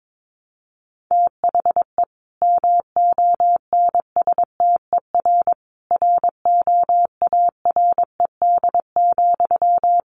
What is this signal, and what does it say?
Morse code, heard as a keyed tone, about 22 wpm, 700 Hz. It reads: THE MONSTER ROARED,
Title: A Sound Of Thunder 22wpm